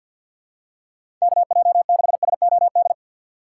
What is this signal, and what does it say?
Morse code, heard as a keyed tone, about 52 wpm, 700 Hz. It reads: KJ6SOD
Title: USA Call Signs - 6 Characters - 3x Character Spacing - 50wpm